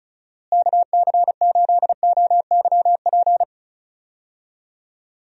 Morse code, heard as a keyed tone, about 36 wpm, 700 Hz. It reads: KC8OYP
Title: USA Call Signs - 6 Characters - Repeated 2x - 35wpm